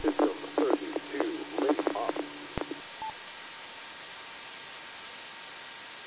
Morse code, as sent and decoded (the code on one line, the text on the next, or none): none